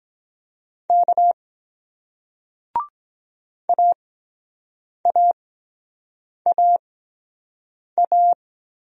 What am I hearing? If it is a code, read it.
Morse code, heard as a keyed tone, about 23 wpm, 700 Hz. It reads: K A A A A